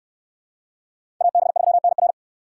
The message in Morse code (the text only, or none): AB3NR